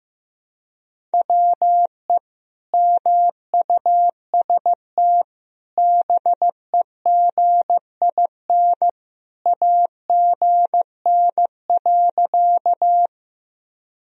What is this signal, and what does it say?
Morse code, heard as a keyed tone, about 15 wpm, 700 Hz. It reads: WE MUST BEGIN AGN.
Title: Sentences from Top 300 Words and Abbreviations 15wpm